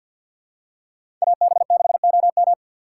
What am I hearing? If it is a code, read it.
Morse code, heard as a keyed tone, about 53 wpm, 700 Hz. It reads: AB6OM
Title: USA Call Signs - 5 Characters - 4x Character Spacing - 50wpm